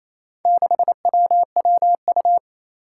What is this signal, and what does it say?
Morse code, heard as a keyed tone, about 28 wpm, 700 Hz. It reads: 6WWU
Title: Four Challenging Letter-Number Combinations 28wpm